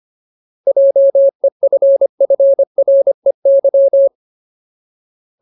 Morse code, heard as a keyed tone, about 25 wpm, 550 Hz. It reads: JEFFREY
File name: Ultimate Word List / 25WPM Ultimate